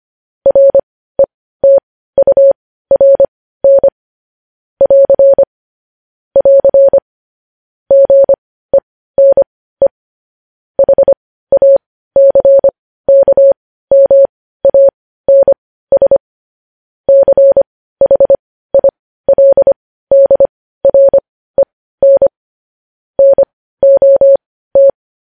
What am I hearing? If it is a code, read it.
Morse code, heard as a keyed tone, about 25 wpm, spaced as for 15 wpm, 550 Hz. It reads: RETURN <AR> <AR> GENE HACKMANS CHILDREN NOT
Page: news headlines in morse code at 15 wpm